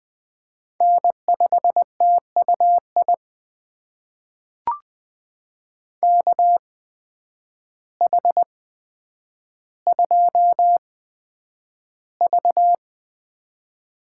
Morse code, heard as a keyed tone, about 20 wpm, spaced as for 9 wpm, 700 Hz. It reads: N5TUI KH2V
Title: USA Call Signs - 5 Characters - 8x Character Spacing - 20wpm